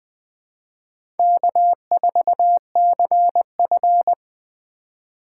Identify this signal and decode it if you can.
Morse code, heard as a keyed tone, about 20 wpm, 700 Hz. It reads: K4CF
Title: USA Call Signs - 4 Characters - 3x Character Spacing - 20wpm